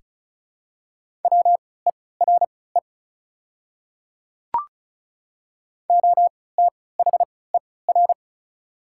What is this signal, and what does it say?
Morse code, heard as a keyed tone, about 36 wpm, spaced as for 20 wpm, 690 Hz. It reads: WERE OTHER
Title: 3x Character Spacing - Top 100 Words - 35wpm